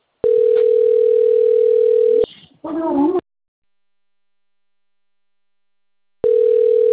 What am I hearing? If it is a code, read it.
Ringback tone (North America), 440 + 480 Hz, with speech.
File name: rg-1600-unknown- - - .2408